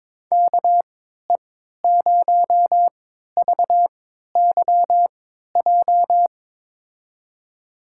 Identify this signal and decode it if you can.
Morse code, heard as a keyed tone, about 21 wpm, spaced as for 12 wpm, 700 Hz. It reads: KE0VYJ
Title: USA Call Signs - 6 Characters - 3x Character Spacing - 22wpm